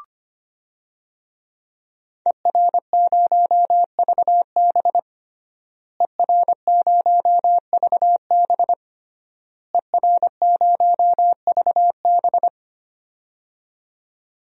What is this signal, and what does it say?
Morse code, heard as a keyed tone, about 25 wpm, 700 Hz. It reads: ER0VB ER0VB ER0VB
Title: DXCC-100 Call Signs - 5 Characters - Repeated 3x - 25wpm